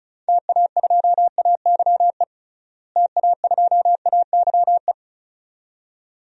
Morse code, heard as a keyed tone, about 36 wpm, 700 Hz. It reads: TA2AYE TA2AYE
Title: DXCC-100 Call Signs - 6 Characters - Repeated 2x - 35wpm